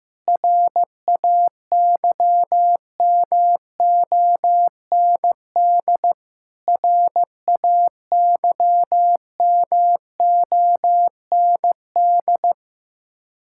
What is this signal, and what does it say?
Morse code, heard as a keyed tone, about 15 wpm, 700 Hz. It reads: RAYMOND RAYMOND
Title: Practice QSO - Exchange 1 - Beginning - 15wpm